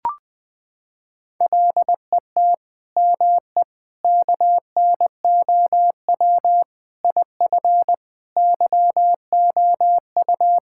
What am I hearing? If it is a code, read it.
Morse code, heard as a keyed tone, about 20 wpm, 700 Hz. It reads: LET ME KNOW IF YOU
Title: Sentences with 8-Word Phrases - 20wpm